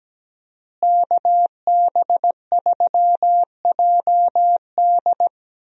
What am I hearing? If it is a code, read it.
Morse code, heard as a keyed tone, about 17 wpm, 700 Hz. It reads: KB3JD
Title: USA Call Signs - 5 Characters - Repeated 3x - 17wpm